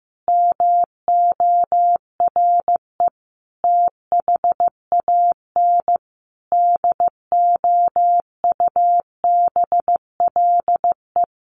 Morse code, heard as a keyed tone, about 14 wpm, 700 Hz. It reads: MORE THAN DOUBLE
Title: Sentences from Top 1400 Words 15wpm